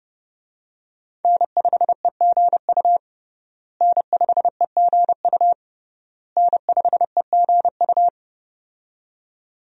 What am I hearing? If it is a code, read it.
Morse code, heard as a keyed tone, about 30 wpm, 700 Hz. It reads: N5EGU N5EGU N5EGU
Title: USA Call Signs - 5 Characters - Repeated 3x - 30wpm